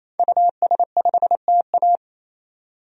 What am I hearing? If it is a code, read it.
Morse code, heard as a keyed tone, about 28 wpm, 700 Hz. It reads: US5TA
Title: DXCC-100 Call Signs - 5 Characters - 28wpm